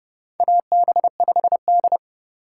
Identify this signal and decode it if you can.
Morse code, heard as a keyed tone, about 31 wpm, 700 Hz. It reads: AB5D
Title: USA Call Signs - 2x Character Spacing - 30wpm